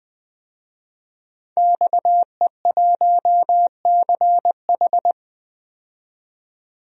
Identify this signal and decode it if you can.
Morse code, heard as a keyed tone, about 20 wpm, 700 Hz. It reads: XE1CH
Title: DXCC-100 Call Signs - 5 Characters - 20wpm